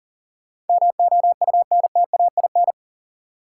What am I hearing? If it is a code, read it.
Morse code, heard as a keyed tone, about 40 wpm, 700 Hz. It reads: MOUNTAIN
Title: Sentences from Top 1300 Words - New Word Drill Alt - 40wpm